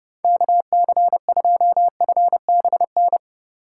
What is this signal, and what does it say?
Morse code, heard as a keyed tone, about 30 wpm, 700 Hz. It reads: KC2FBN